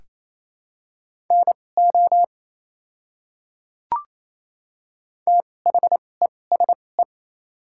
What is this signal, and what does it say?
Morse code, heard as a keyed tone, about 28 wpm, spaced as for 20 wpm, 690 Hz. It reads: NO THESE
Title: 2x Character Spacing - Top 100 Words - 28wpm